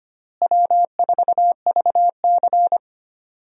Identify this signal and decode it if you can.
Morse code, heard as a keyed tone, about 25 wpm, 700 Hz. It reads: W4VC
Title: USA Call Signs - 4 Characters - Repeated 3x - 25wpm